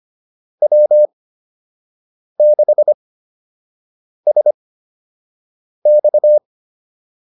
Morse code, heard as a keyed tone, about 25 wpm, spaced as for 6 wpm, 600 Hz. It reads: W6SX
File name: QSO 102 06